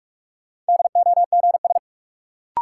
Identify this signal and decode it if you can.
Morse code, heard as a keyed tone, about 41 wpm, 700 Hz. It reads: DOGS
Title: Sentences from Top 1600 Words - New Word Drill - 45wpm